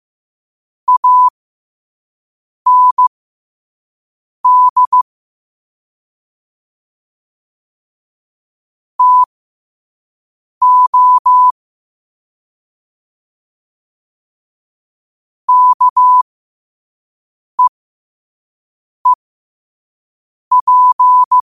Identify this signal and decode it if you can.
Morse code, heard as a keyed tone, about 15 wpm, spaced as for 5 wpm, 1000 Hz. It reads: AND TO KEEP
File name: qotdfiles / 5wpm